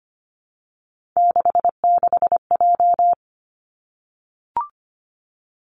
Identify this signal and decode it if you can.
Morse code, heard as a keyed tone, about 25 wpm, 700 Hz. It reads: 66J
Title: Three Challenging Letter-Number Combinations 25wpm